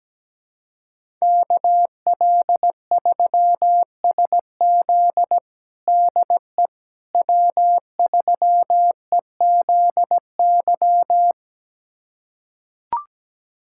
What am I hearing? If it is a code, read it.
Morse code, heard as a keyed tone, about 17 wpm, 700 Hz. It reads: KL3SZ DE W3EZY
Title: USA Callsign DE Callsign - 5 Characters - 17wpm